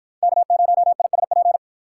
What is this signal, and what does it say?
Morse code, heard as a keyed tone, about 53 wpm, 700 Hz. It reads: K0ISP